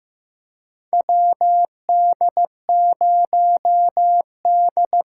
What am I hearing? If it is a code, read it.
Morse code, heard as a keyed tone, about 15 wpm, 700 Hz. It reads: WD0D
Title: USA Call Signs - 6 Characters - 8x Character Spacing - 15wpm